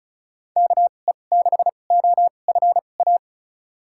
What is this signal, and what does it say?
Morse code, heard as a keyed tone, about 36 wpm, spaced as for 25 wpm, 700 Hz. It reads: KE6OFA
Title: USA Call Signs - 6 Characters - 2x Character Spacing - 35wpm